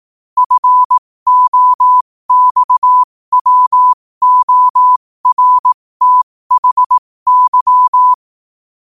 Morse code, heard as a keyed tone, about 18 wpm, spaced as for 15 wpm, 1000 Hz. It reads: FOXWORTHY
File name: 15wpm